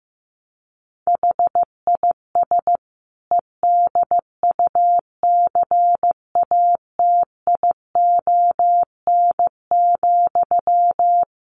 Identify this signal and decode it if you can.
Morse code, heard as a keyed tone, about 14 wpm, 700 Hz. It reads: HIS EDUCATION,
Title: Sentences from Top 1400 Words 15wpm